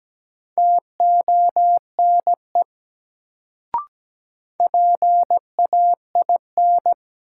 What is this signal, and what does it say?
Morse code, heard as a keyed tone, about 17 wpm, 700 Hz. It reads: TONE PAIN
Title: Sentences from Top 1600 Words - New Word Drill - 17wpm